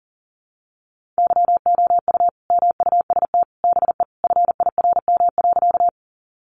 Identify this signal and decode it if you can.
Morse code, heard as a keyed tone, about 40 wpm, 700 Hz. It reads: YOU MUST BE FIRM.
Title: Sentences from Top 1400 Words 40wpm